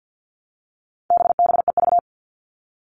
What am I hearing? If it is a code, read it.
Morse code, heard as a keyed tone, about 50 wpm, 700 Hz. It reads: BBEV
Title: All Cut Numbers - 4-Digit 50wpm